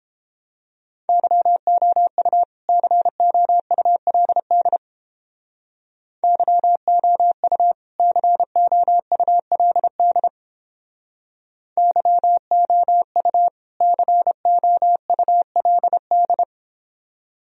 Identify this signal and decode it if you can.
Morse code, heard as a keyed tone, about 31 wpm, 700 Hz. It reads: YOU COULD YOU COULD YOU COULD